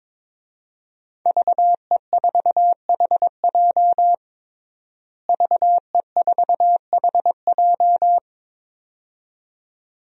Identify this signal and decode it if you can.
Morse code, heard as a keyed tone, about 22 wpm, 700 Hz. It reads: VE4HJ VE4HJ
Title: DXCC-100 Call Signs - 5 Characters - Repeated 2x - 22wpm